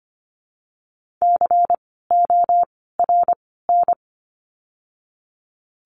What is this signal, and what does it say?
Morse code, heard as a keyed tone, about 25 wpm, 700 Hz. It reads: C O R N
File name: course / 25:10wpm / 094 Lesson 16f - Letter c - Words 25wpm